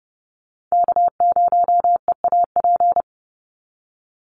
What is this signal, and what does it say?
Morse code, heard as a keyed tone, about 31 wpm, 700 Hz. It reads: K0EAP
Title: NA SOTA Chaser Callsigns 30wpm